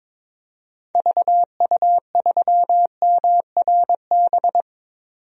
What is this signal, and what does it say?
Morse code, heard as a keyed tone, about 22 wpm, 700 Hz. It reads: VU3MRB
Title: DXCC-100 Call Signs - 6 Characters - 22wpm